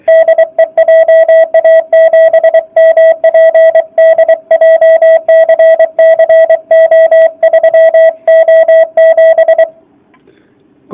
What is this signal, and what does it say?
Morse code, heard as a keyed tone, about 24 wpm, 660 Hz. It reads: DE JA7MPD JCC O3O7